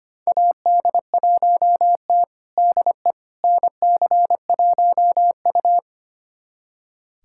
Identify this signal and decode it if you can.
Morse code, heard as a keyed tone, about 25 wpm, 700 Hz. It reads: AD1T DE NC1U
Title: USA Callsign DE Callsign - 4 Characters - 25wpm